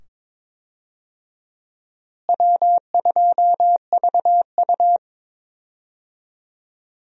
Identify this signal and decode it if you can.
Morse code, heard as a keyed tone, about 22 wpm, 700 Hz. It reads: W2VU
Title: USA Call Signs - 4 Characters - Repeated 2x - 22wpm